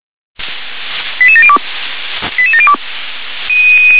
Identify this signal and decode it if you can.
Five-tone selective call (ZVEI), heard as quick runs of five tones, tones of about 70 ms. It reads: 88702 88702